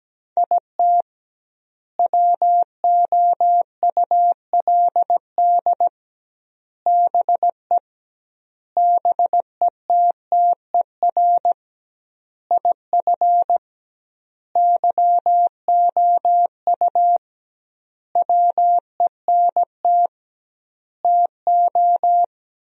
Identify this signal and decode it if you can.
Morse code, heard as a keyed tone, about 17 wpm, 700 Hz. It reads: IT WOULD BE BETTER IF YOU WENT TO
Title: 2x Word-Spacing - Sentences from Top 400 Words - 17wpm